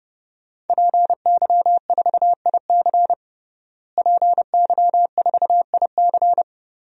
Morse code, heard as a keyed tone, about 29 wpm, 700 Hz. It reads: PY4IC PY4IC
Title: DXCC-100 Call Signs - 5 Characters - Repeated 2x - 30wpm